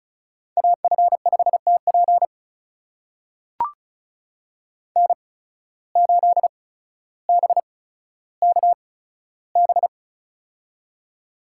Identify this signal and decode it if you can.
Morse code, heard as a keyed tone, about 34 wpm, spaced as for 15 wpm, 700 Hz. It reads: AF5TP N8BKB
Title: USA Call Signs - 5 Characters - 8x Character Spacing - 35wpm